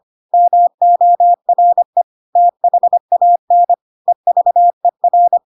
Morse code, heard as a keyed tone, about 25 wpm, 700 Hz. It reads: MORE THAN EVER
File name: Pinocchio / Pinocchio ch5 25 25 ik1hge